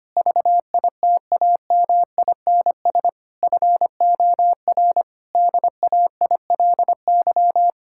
Morse code, heard as a keyed tone, about 25 wpm, 700 Hz. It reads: VITAMINS FOR DAILY
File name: practice / 25wpm / Difficult Sentences with Hints - 25wpm